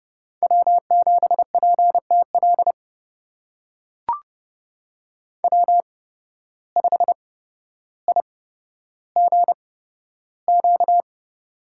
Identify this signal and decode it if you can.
Morse code, heard as a keyed tone, about 30 wpm, spaced as for 13 wpm, 700 Hz. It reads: W7PTL W5IGQ
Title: USA Call Signs - 8x Character Spacing - 30wpm